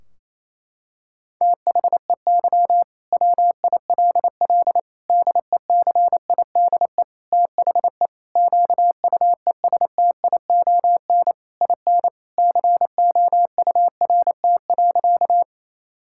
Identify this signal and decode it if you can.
Morse code, heard as a keyed tone, about 28 wpm, 700 Hz. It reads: THEY WILL DECIDE THE QUESTION IN COURT.
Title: Sentences from Top 1600 Words 28wpm